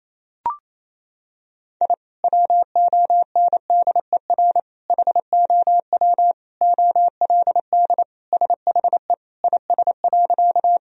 Morse code, heard as a keyed tone, about 28 wpm, 700 Hz. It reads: I WONDER HOW OLD SHE IS.
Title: Sentences from Top 1000 Words - Review Part 3 - 28wpm